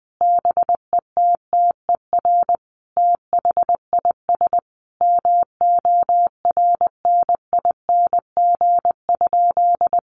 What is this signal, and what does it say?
Morse code, heard as a keyed tone, about 20 wpm, 700 Hz. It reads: BETTER THIS MORNING?
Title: Sentences from Top 1200 Words 20wpm